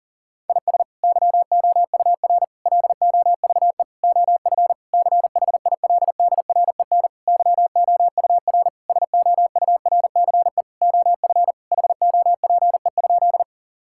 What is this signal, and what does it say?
Morse code, heard as a keyed tone, about 40 wpm, 700 Hz. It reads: IS YOUR LOVE OF CHILDREN YOUR SOURCE OF HOPE?
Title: Sentences from Top 1000 Words - Review Part 5 - 40wpm